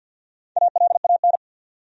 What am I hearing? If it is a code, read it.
Morse code, heard as a keyed tone, about 48 wpm, 700 Hz. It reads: ALAN
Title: USA Common Names - Max 4 Letters - 50wpm